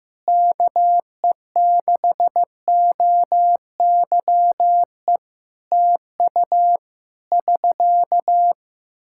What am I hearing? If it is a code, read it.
Morse code, heard as a keyed tone, about 15 wpm, 700 Hz. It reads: KE6OYE TU <SK>
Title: Practice QSO - Exchange 4 - Ending - 15wpm